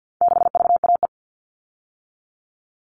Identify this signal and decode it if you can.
Morse code, heard as a keyed tone, about 50 wpm, 700 Hz. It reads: 6VAE
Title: All Cut Numbers - 4-Digit 50wpm